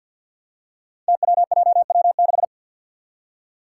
Morse code, heard as a keyed tone, about 48 wpm, 700 Hz. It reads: TWJW6
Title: Five Challenging Letter-Number Combinations 50wpm